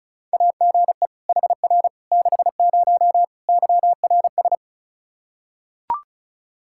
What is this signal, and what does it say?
Morse code, heard as a keyed tone, about 34 wpm, 700 Hz. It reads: AGE HR 60 YRS
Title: Practice QSO - Exchange 3 - More Info - 35wpm